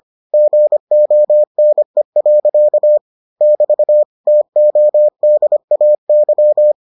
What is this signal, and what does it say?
Morse code, heard as a keyed tone, about 25 wpm, 600 Hz. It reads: GONE. <BT> TODAY